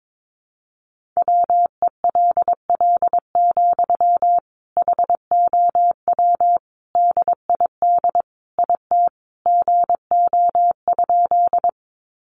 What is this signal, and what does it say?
Morse code, heard as a keyed tone, about 22 wpm, 700 Hz. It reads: WELL, HOW DID IT GO?
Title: Sentences from Top 200 Words 22wpm